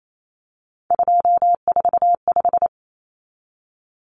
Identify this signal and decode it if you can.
Morse code, heard as a keyed tone, about 28 wpm, 700 Hz. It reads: 245
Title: Common Cut Numbers - 3-Digit 28wpm